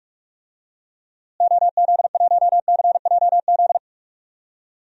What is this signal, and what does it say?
Morse code, heard as a keyed tone, about 46 wpm, 700 Hz. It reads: OZ1CJZ
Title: DXCC-100 Call Signs - 6 Characters - Repeated 3x - 45wpm